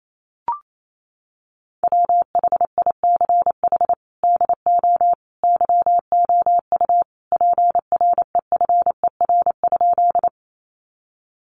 Morse code, heard as a keyed tone, about 28 wpm, 700 Hz. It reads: WHICH DO YOU PREFER?
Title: Sentences from Top 1100 Words 28wpm